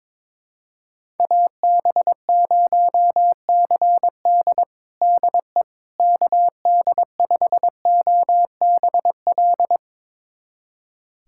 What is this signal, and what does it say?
Morse code, heard as a keyed tone, about 22 wpm, 700 Hz. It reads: AB0CD DE KD5OBL
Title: USA Callsign DE Callsign 22wpm